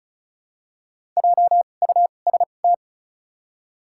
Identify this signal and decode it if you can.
Morse code, heard as a keyed tone, about 34 wpm, spaced as for 25 wpm, 700 Hz. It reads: JUST
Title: 2x Character Spacing - Top 300 Words - 35wpm